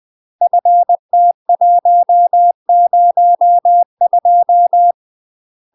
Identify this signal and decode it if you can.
Morse code, heard as a keyed tone, about 20 wpm, 700 Hz. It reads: FT102
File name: UK MASTER QSOs TRAINING / 20WPM / UK MASTER QSOs 0015